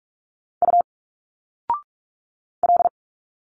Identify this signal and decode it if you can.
Morse code, heard as a keyed tone, about 45 wpm, 700 Hz. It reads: U L
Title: Single Letters 45wpm